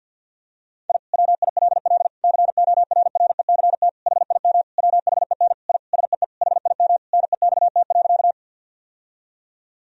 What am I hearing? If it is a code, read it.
Morse code, heard as a keyed tone, about 50 wpm, 700 Hz. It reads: I WILL CORRECT HIM WHEN I SEE HIM NEXT.
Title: Sentences from Top 1000 Words - Review Part 2 - 50wpm